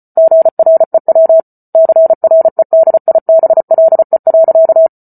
Morse code, heard as a keyed tone, about 34 wpm, 650 Hz. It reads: GREW CREDIBLE.
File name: Book Two / 08-Chapter 8 - 34 WPM